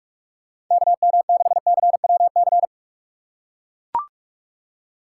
Morse code, heard as a keyed tone, about 43 wpm, 700 Hz. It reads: KM6CWC
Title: USA Call Signs - 6 Characters - 2x Character Spacing - 45wpm